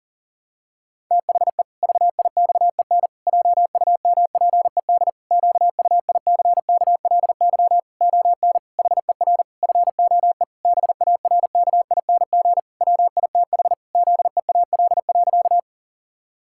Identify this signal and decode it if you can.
Morse code, heard as a keyed tone, about 37 wpm, 700 Hz. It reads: THE VIXEN JUMPED QUICKLY ON HER FOE BARKING WITH ZEAL.
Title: Pangram Challenge v2 - 40wpm